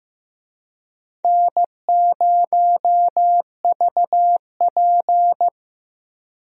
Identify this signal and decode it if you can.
Morse code, heard as a keyed tone, about 15 wpm, 700 Hz. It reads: N0VP
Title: USA Call Signs - 4 Characters - Repeated 2x - 15wpm